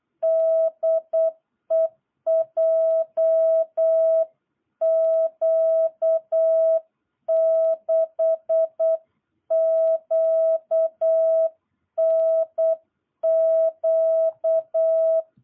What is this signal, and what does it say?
Morse code, heard as a keyed tone, about 8 wpm, 650 Hz. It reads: DE JQ6QNQ
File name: jh8cbh / 2022cw